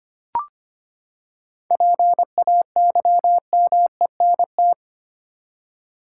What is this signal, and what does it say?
Morse code, heard as a keyed tone, about 25 wpm, 700 Hz. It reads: PAYMENT
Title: Seven Letter Words - 25wpm